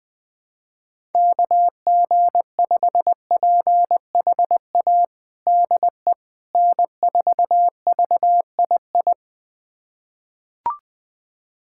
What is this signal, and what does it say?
Morse code, heard as a keyed tone, about 21 wpm, 700 Hz. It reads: KG5PHA DE N4VII
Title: USA Callsign DE Callsign 20wpm